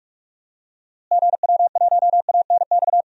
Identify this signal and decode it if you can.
Morse code, heard as a keyed tone, about 46 wpm, 700 Hz. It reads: GW1ANX
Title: DXCC-100 Call Signs - 6 Characters - Repeated 3x - 45wpm